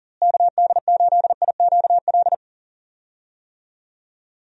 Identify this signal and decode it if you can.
Morse code, heard as a keyed tone, about 40 wpm, 700 Hz. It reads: KD8IQL
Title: USA Call Signs - 6 Characters - Repeated 3x - 40wpm